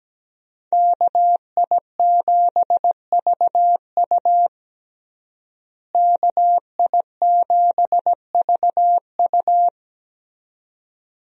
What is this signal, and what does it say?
Morse code, heard as a keyed tone, about 17 wpm, 700 Hz. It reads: KI7VU KI7VU
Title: USA Call Signs - 5 Characters - Repeated 2x - 17wpm